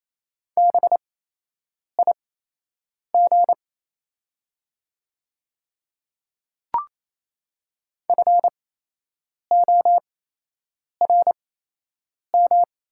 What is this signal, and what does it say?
Morse code, heard as a keyed tone, about 28 wpm, spaced as for 8 wpm, 700 Hz. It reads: BIG FORM
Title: 8x Character Spacing - Top 200 Words - 28wpm